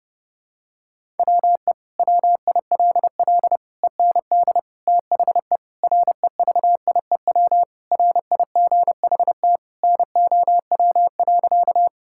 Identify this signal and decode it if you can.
Morse code, heard as a keyed tone, about 32 wpm, 700 Hz. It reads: WE WILL END THE REVIEW RIGHT NOW.
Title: Sentences from Top 1000 Words - Review Part 6 - 30wpm